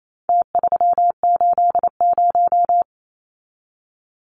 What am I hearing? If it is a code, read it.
Morse code, heard as a keyed tone, about 28 wpm, 700 Hz. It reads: T380
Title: Numbers - Contest Style - 4-Digit 28wpm